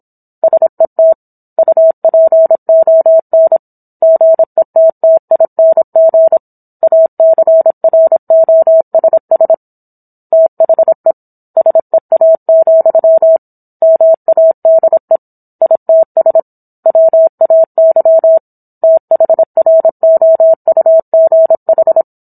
Morse code, heard as a keyed tone, about 27 wpm, 650 Hz. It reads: SET UPON GETTING ACROSS THE SEA, MADE ITS WAY THROUGH